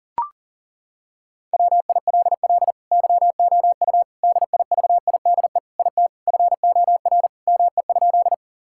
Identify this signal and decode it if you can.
Morse code, heard as a keyed tone, about 40 wpm, 700 Hz. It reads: WILL YOU DIVIDE IT FOR ME?
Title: Sentences from Top 1600 Words 40wpm